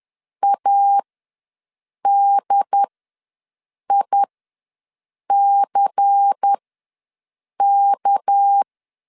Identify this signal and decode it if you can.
Morse code, heard as a keyed tone, about 10 wpm, spaced as for 6 wpm, 800 Hz. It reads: ADICK
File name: LEZIONI / 8 YJ45 / 8 Y ws 10wpm 3952